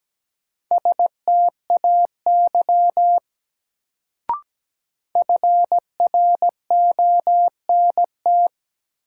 Morse code, heard as a keyed tone, about 17 wpm, 700 Hz. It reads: STAY FRONT
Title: Sentences from Top 500 Words - New Word Drill - 17wpm